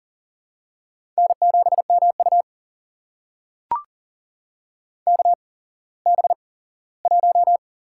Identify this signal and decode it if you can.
Morse code, heard as a keyed tone, about 37 wpm, spaced as for 19 wpm, 700 Hz. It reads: N7MU KB1
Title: USA Call Signs - 8x Character Spacing - 40wpm